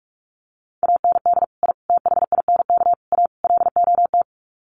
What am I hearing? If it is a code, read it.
Morse code, heard as a keyed tone, about 44 wpm, 700 Hz. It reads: AND I THINK A LOT
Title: Sets of 5 Words Encore2 45wpm